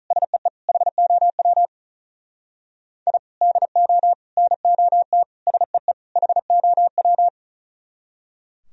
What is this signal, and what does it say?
Morse code, heard as a keyed tone, about 43 wpm, 700 Hz. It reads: SEE HOW I DO NOT SEE HOW
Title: Speed-Racing - Sets of 5 Words 35wpm